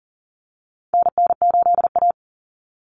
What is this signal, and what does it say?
Morse code, heard as a keyed tone, about 40 wpm, 700 Hz. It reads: NN8A
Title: Common Cut Numbers - 4-Digit 40wpm